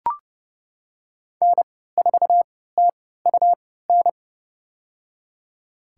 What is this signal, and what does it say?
Morse code, heard as a keyed tone, about 31 wpm, spaced as for 17 wpm, 700 Hz. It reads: N4TUN
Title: USA Call Signs - 5 Characters - 3x Character Spacing - 30wpm